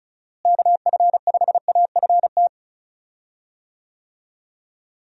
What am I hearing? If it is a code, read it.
Morse code, heard as a keyed tone, about 34 wpm, 700 Hz. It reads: KF5AFT
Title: USA Call Signs - 6 Characters - Repeated 3x - 35wpm